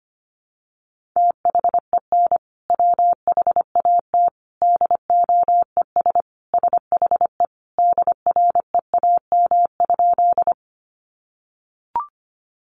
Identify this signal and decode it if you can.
Morse code, heard as a keyed tone, about 25 wpm, 700 Hz. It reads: THEN WHAT DOES SHE DREAM?
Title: Sentences from Top 1500 Words 25wpm